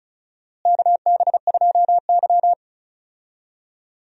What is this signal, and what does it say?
Morse code, heard as a keyed tone, about 36 wpm, 700 Hz. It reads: KB2Y
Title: USA Call Signs - 4 Characters - Repeated 3x - 35wpm